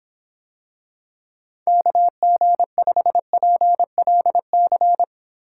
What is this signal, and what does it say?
Morse code, heard as a keyed tone, about 26 wpm, 700 Hz. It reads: KG5PLC